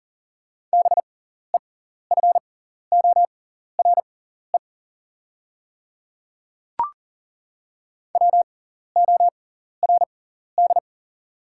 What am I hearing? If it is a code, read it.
Morse code, heard as a keyed tone, about 40 wpm, spaced as for 14 wpm, 700 Hz. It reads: BEFORE WORD
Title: 6x Character Spacing - Top 200 Words - 40wpm